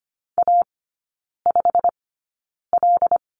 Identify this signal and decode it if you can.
Morse code, heard as a keyed tone, about 25 wpm, spaced as for 9 wpm, 700 Hz. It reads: A5L
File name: Lesson 15e - Number 5 - Three Characters 25wpm